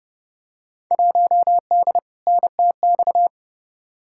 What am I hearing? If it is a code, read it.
Morse code, heard as a keyed tone, about 30 wpm, 700 Hz. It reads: 1D NTX
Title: Field Day Exchanges - 30wpm